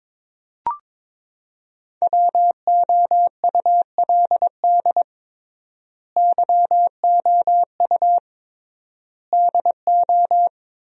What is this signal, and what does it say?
Morse code, heard as a keyed tone, about 22 wpm, 700 Hz. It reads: WOULD YOU DO
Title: 3x Word-Spacing - Sentences from Top 100 Words - 22wpm